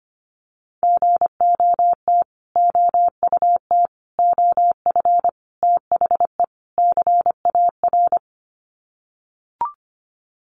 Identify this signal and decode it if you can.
Morse code, heard as a keyed tone, about 25 wpm, 700 Hz. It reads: GOT OUT OF THE CAR
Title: Sets of 5 Words Encore 25wpm